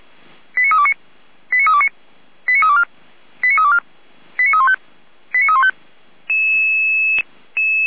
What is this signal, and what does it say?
Five-tone selective call (ZVEI), heard as quick runs of five tones, tones of about 70 ms. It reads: 89329 89329 89324 89324 89315 89315